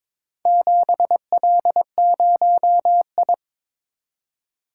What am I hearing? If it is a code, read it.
Morse code, heard as a keyed tone, about 22 wpm, 700 Hz. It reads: 7L0I
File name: practice / 22wpm / Four Letter-Number Combinations - 22wpm